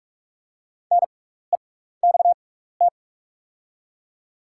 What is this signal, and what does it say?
Morse code, heard as a keyed tone, about 44 wpm, spaced as for 15 wpm, 700 Hz. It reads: NEXT
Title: 6x Character Spacing - Top 500 Words - 45wpm